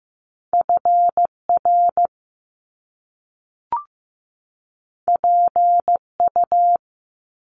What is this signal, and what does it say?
Morse code, heard as a keyed tone, about 15 wpm, 700 Hz. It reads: FR PU
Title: Top Two Letter Combinations in Words 15wpm